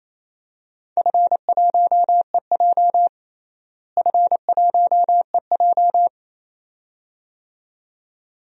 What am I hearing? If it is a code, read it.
Morse code, heard as a keyed tone, about 27 wpm, 700 Hz. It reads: F1EJ F1EJ
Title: DXCC-100 Call Signs - 4 Characters - Repeated 2x - 28wpm